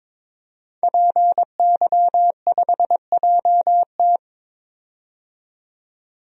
Morse code, heard as a keyed tone, about 21 wpm, 700 Hz. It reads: PY5JT
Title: DXCC-100 Call Signs - 5 Characters - 22wpm